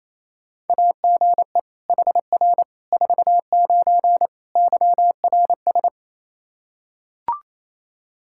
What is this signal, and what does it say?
Morse code, heard as a keyed tone, about 27 wpm, 700 Hz. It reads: AGE HR 49 YRS
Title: K1USN SST QSO Practice 28wpm